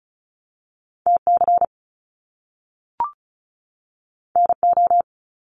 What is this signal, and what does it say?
Morse code, heard as a keyed tone, about 36 wpm, 700 Hz. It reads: TC NO
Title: Top Two Letter Combinations in Words 35wpm